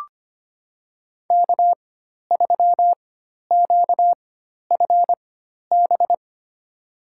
Morse code, heard as a keyed tone, about 25 wpm, spaced as for 12 wpm, 700 Hz. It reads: K3QFB
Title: USA Call Signs - 5 Characters - 4x Character Spacing - 25wpm